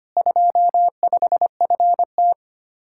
Morse code, heard as a keyed tone, about 33 wpm, 700 Hz. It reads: 25FT